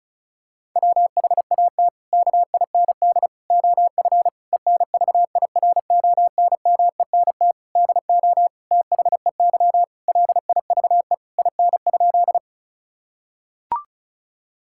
Morse code, heard as a keyed tone, about 36 wpm, 700 Hz. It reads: WHAT KIND OF ENVIRONMENT DO THEY LIVE IN?
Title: Sentences from Top 1000 Words - Review Part 5 - 35wpm